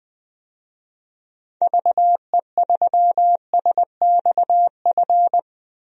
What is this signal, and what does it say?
Morse code, heard as a keyed tone, about 20 wpm, 700 Hz. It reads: VE3SXF
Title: DXCC-100 Call Signs - 6 Characters - Repeated 2x - 20wpm